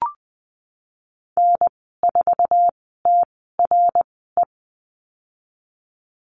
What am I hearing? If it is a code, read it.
Morse code, heard as a keyed tone, about 20 wpm, spaced as for 15 wpm, 700 Hz. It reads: N4TRE
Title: 055 Lesson 10g - Letter r - Call Signs 20wpm